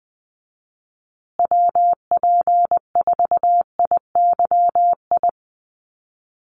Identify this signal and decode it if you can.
Morse code, heard as a keyed tone, about 20 wpm, 700 Hz. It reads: WP4IYI
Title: USA Call Signs Repeated 3x 20wpm